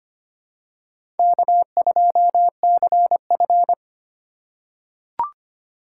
Morse code, heard as a keyed tone, about 25 wpm, 700 Hz. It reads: K2CF
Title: USA Call Signs - 4 Characters - 3x Character Spacing - 25wpm